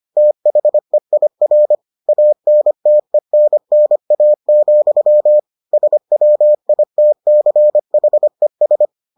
Morse code, heard as a keyed tone, about 25 wpm, 600 Hz. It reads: THEIR ANTENNA, SWITCHES